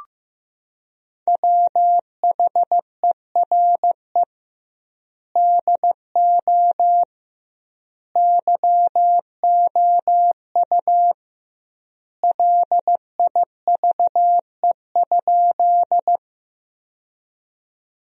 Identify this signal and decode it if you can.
Morse code, heard as a keyed tone, about 15 wpm, 700 Hz. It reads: WHERE DO YOU LIVE?
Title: 2x Word-Spacing - Sentences from Top 200 Words - 15wpm